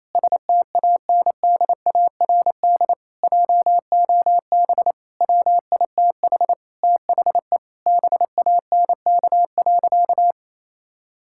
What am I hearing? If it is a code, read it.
Morse code, heard as a keyed tone, about 28 wpm, 700 Hz. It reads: STANDARD JOB WITH THE BANK.
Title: Sentences from Top 1000 Words - Review Part 6 - 28wpm